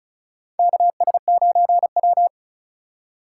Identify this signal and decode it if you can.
Morse code, heard as a keyed tone, about 34 wpm, 700 Hz. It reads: KS9W
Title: K1USN SST QSO Practice - 35wpm